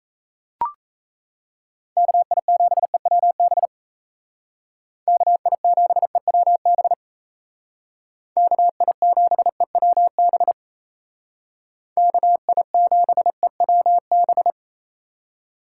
Morse code, heard as a keyed tone, about 34 wpm, 700 Hz. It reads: KI7EWB KI7EWB KI7EWB KI7EWB
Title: Speed Racing - USA 2x3 Call Signs 28wpm